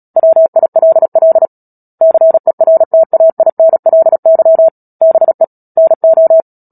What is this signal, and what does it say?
Morse code, heard as a keyed tone, about 37 wpm, 650 Hz. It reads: WILL CERTAINLY BE NO
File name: Book Two / 10-Chapter 10 - Epilogue - 36 WPM